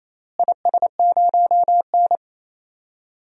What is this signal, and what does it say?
Morse code, heard as a keyed tone, about 28 wpm, 700 Hz. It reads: IS0N
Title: DXCC-100 Call Signs - 4 Characters - Repeated 2x - 28wpm